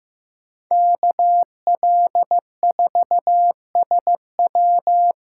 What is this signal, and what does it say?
Morse code, heard as a keyed tone, about 18 wpm, 700 Hz. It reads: KL4SW
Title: Speed Racing - USA 2x2 Call Signs 15wpm